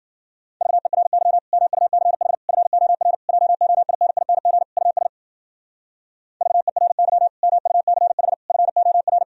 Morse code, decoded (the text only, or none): VERY MUCH FOR JOINING US VERY MUCH FOR